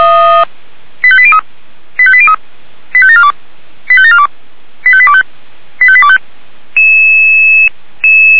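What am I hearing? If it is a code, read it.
Five-tone selective call (ZVEI), heard as quick runs of five tones, tones of about 70 ms. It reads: 85803 85803 85742 85742 85825 85825
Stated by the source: ZVEI five-tone selective call received on an SDR.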